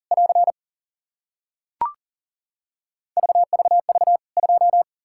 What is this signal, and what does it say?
Morse code, heard as a keyed tone, about 40 wpm, 700 Hz. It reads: <AR> VVV 2
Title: K1USN SST QSO Practice 40wpm